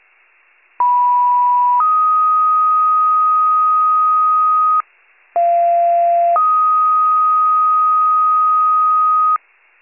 Two-tone page, heard as a one-second tone, then a three-second tone, about 980 then 1250 Hz, about 690 then 1250 Hz. It is heard from 0.8 to 9.4 s.